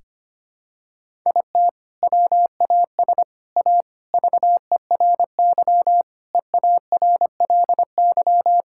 Morse code, heard as a keyed tone, about 25 wpm, 700 Hz. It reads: IT WAS A VERY EARLY
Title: Sentences from Top 1000 Words - Review Part 6 - 25wpm